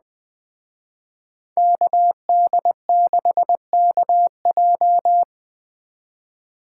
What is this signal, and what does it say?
Morse code, heard as a keyed tone, about 21 wpm, 700 Hz. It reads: KD6KJ